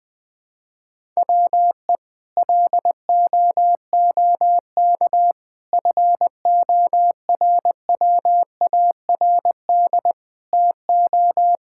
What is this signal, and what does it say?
Morse code, heard as a keyed tone, about 20 wpm, 700 Hz. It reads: WE LOOK FORWARD TO
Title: Sets of 6 Words - 20wpm